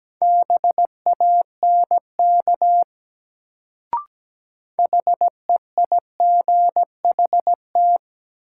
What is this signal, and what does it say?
Morse code, heard as a keyed tone, about 17 wpm, 700 Hz. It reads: BANK HEIGHT
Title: Sets of 5 Words - New Word Drill - 17wpm